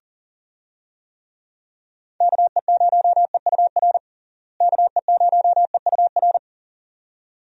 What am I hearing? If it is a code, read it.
Morse code, heard as a keyed tone, about 40 wpm, 700 Hz. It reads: KE0EUR KE0EUR
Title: USA Call Signs - 6 Characters - Repeated 2x - 40wpm